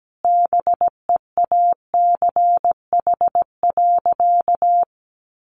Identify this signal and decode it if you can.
Morse code, heard as a keyed tone, about 17 wpm, 700 Hz. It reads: BEACH.
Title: Sentences from Top 1500 Words 17wpm